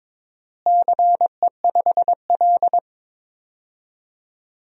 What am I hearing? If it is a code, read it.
Morse code, heard as a keyed tone, about 22 wpm, 700 Hz. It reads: CE5L
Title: DXCC-100 Call Signs - 4 Characters - Repeated 3x - 22wpm